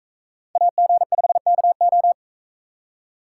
Morse code, heard as a keyed tone, about 42 wpm, 700 Hz. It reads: AG5KO